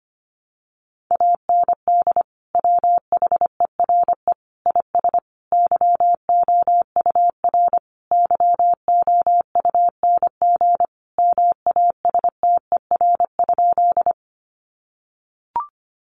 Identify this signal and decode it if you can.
Morse code, heard as a keyed tone, about 25 wpm, 700 Hz. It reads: AND WHERE IS YOUR YOUNG MASTER?
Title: Sentences from Top 1300 Words 25wpm